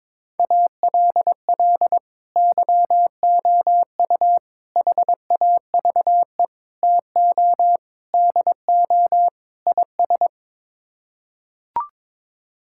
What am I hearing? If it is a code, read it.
Morse code, heard as a keyed tone, about 22 wpm, 700 Hz. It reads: ALL YOU HAVE TO DO IS
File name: practice / 22wpm / Sets of 6 Words - 22wpm